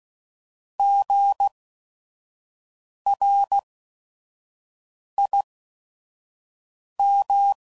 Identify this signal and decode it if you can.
Morse code, heard as a keyed tone, about 16 wpm, spaced as for 5 wpm, 800 Hz. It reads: GRIM
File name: Prefixes / By First Letter / Gs